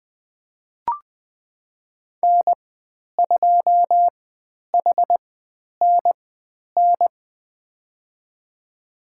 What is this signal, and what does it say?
Morse code, heard as a keyed tone, about 20 wpm, spaced as for 10 wpm, 700 Hz. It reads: N2HNN
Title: 109 Lesson 18g - Letter m - Call Signs 20wpm